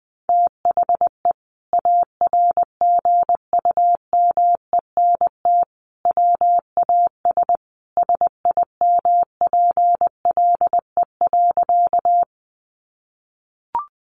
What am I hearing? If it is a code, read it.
Morse code, heard as a keyed tone, about 20 wpm, 700 Hz. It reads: THE ARGUMENT WAS SIMPLE.
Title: Sentences from Top 1400 Words 20wpm